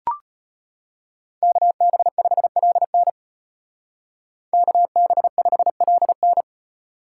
Morse code, heard as a keyed tone, about 33 wpm, 700 Hz. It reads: KB5LN KB5LN